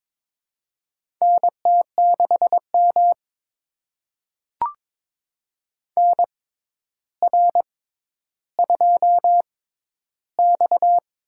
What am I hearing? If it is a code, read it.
Morse code, heard as a keyed tone, about 22 wpm, spaced as for 11 wpm, 700 Hz. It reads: NT6M NR2X
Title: USA Call Signs - 4 Characters - 6x Character Spacing - 22wpm